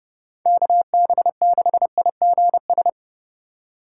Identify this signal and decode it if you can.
Morse code, heard as a keyed tone, about 31 wpm, 700 Hz. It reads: KB6IGS
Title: USA Call Signs - 2x Character Spacing - 30wpm